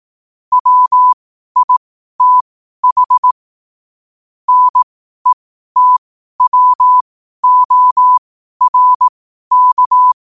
Morse code, heard as a keyed tone, about 18 wpm, spaced as for 12 wpm, 1000 Hz. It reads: WITH NETWORK
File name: qotdfiles / 12wpm